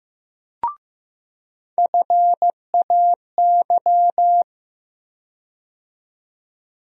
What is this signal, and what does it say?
Morse code, heard as a keyed tone, about 15 wpm, 700 Hz. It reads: FAY
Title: USA Call Sign Postfixes - 3 Letters 15wpm